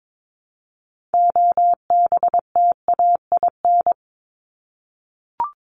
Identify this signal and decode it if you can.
Morse code, heard as a keyed tone, about 22 wpm, 700 Hz. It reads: OBTAIN
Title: ED Suffix - Top 100 Words with Base Words 22wpm